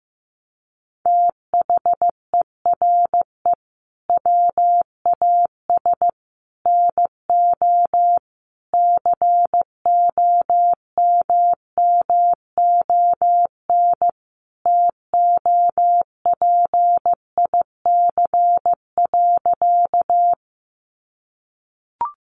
Morse code, heard as a keyed tone, about 15 wpm, 700 Hz. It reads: THERE WAS NO COMMON TOPIC.
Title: Sentences from Top 1200 Words 15wpm